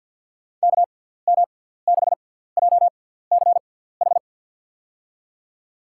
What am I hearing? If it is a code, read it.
Morse code, heard as a keyed tone, about 52 wpm, spaced as for 17 wpm, 700 Hz. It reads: KM6JCH
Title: USA Call Signs - 6 Characters - 6x Character Spacing - 50wpm